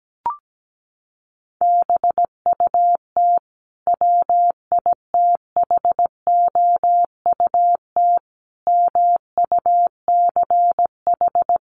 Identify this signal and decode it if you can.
Morse code, heard as a keyed tone, about 17 wpm, 700 Hz. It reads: BUT WITHOUT MUCH
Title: Sentences from Top 900 Words 17wpm